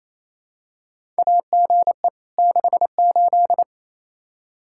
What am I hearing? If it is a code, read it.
Morse code, heard as a keyed tone, about 28 wpm, 700 Hz. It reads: AGE 68
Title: Practice QSO - Exchange 3 - More Info - 28wpm